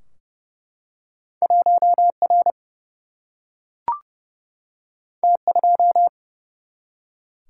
Morse code, heard as a keyed tone, about 30 wpm, 700 Hz. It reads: 1R T2
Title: Single-Letter Single-Number 30wpm